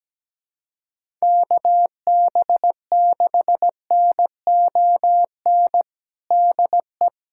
Morse code, heard as a keyed tone, about 17 wpm, 700 Hz. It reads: KB6NON DE
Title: USA Callsign DE Callsign 17wpm